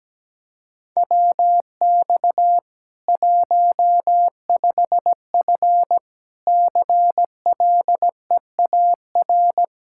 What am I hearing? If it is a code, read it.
Morse code, heard as a keyed tone, about 17 wpm, 700 Hz. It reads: WX 15F CLEAR
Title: Practice QSO - Exchange 2 - Info - 17wpm